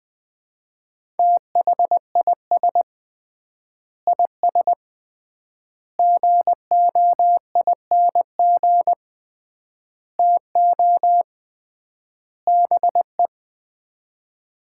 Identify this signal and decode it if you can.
Morse code, heard as a keyed tone, about 20 wpm, 700 Hz. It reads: THIS IS GOING TO BE
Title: 3x Word-Spacing - Sets of 5 Words - 20wpm